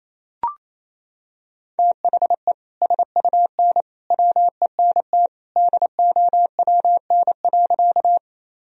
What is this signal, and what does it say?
Morse code, heard as a keyed tone, about 28 wpm, 700 Hz. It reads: THE SUN WENT DOWN.
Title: Sentences from Top 1000 Words - Review Part 1 - 28wpm